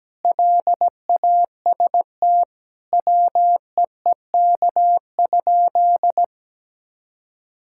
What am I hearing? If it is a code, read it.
Morse code, heard as a keyed tone, about 17 wpm, 700 Hz. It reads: LAST WEEK?
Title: Sentences from Top 1000 Words - Review Part 5 - 17wpm